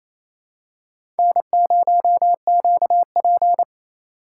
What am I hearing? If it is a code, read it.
Morse code, heard as a keyed tone, about 28 wpm, 700 Hz. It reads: N0QP
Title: USA Call Signs - 4 Characters - 6x Character Spacing - 28wpm